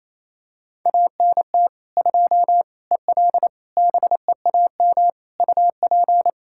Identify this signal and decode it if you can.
Morse code, heard as a keyed tone, about 28 wpm, 700 Hz. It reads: ANT 2 EL BEAM UP
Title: K1USN SST QSO Practice 28wpm